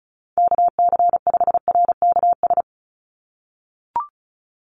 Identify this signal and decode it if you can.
Morse code, heard as a keyed tone, about 34 wpm, 700 Hz. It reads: KC5RKS
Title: USA Call Signs 35wpm